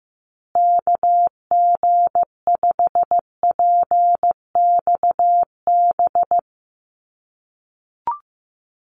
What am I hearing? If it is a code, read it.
Morse code, heard as a keyed tone, about 16 wpm, 700 Hz. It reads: KG5PXB
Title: USA Call Signs Repeated 2x 15wpm